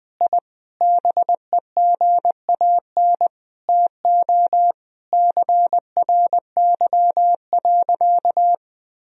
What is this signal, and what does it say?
Morse code, heard as a keyed tone, about 20 wpm, 700 Hz. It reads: I BEGAN TO CRY.
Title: Sentences from Top 1000 Words - Review Part 2 - 20wpm